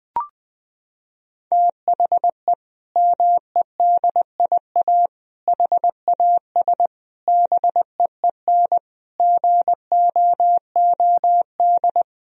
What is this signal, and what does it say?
Morse code, heard as a keyed tone, about 20 wpm, 700 Hz. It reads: THE MEDIA HAS BEEN GOOD
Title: Sentences from Top 1000 Words - Review Part 4 - 20wpm